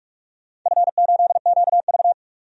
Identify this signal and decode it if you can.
Morse code, heard as a keyed tone, about 43 wpm, 700 Hz. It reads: F8QV
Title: DXCC-100 Call Signs - 4 Characters - 45wpm